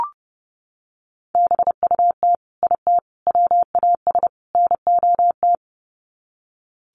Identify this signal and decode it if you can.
Morse code, heard as a keyed tone, about 30 wpm, 700 Hz. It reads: BUT IT WAS NOT
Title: Sets of 4 Words 30wpm